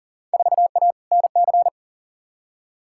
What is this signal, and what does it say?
Morse code, heard as a keyed tone, about 40 wpm, 700 Hz. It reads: 4A NC
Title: Field Day Exchanges Encore - 40wpm